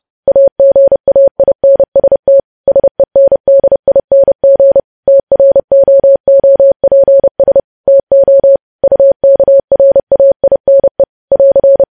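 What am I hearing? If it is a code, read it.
Morse code, heard as a keyed tone, about 30 wpm, 550 Hz. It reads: AGAINST SENDING TROOPS TO UKRAINE <AR>
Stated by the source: Morse code sending of news headlines at 30 wpm.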